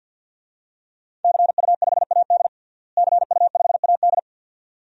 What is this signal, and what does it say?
Morse code, heard as a keyed tone, about 46 wpm, 700 Hz. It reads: CU5AD CU5AD
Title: DXCC-100 Call Signs - 5 Characters - Repeated 2x - 50wpm